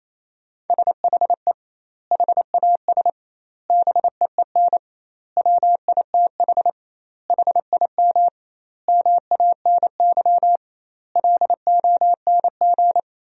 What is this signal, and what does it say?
Morse code, heard as a keyed tone, about 28 wpm, 700 Hz. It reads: SHE HAS BEEN WITH HIM MANY LONG
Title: 2x Word-Spacing - Sentences from Top 200 Words - 28wpm